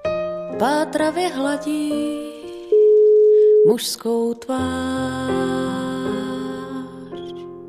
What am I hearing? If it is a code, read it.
Ringback tone (Europe, 425 Hz), with noise.